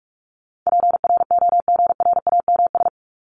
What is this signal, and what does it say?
Morse code, heard as a keyed tone, about 44 wpm, 700 Hz. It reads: PROGRAMS
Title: ES:S Suffix - Top 100 Words with Base Words 45wpm